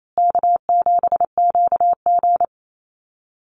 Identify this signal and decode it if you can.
Morse code, heard as a keyed tone, about 28 wpm, 700 Hz. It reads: K7QG
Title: USA Call Signs Repeated 3x 28wpm